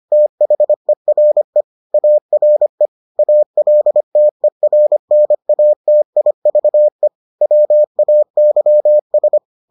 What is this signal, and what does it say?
Morse code, heard as a keyed tone, about 25 wpm, 600 Hz. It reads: THERE ARE ALTERNATIVE WAYS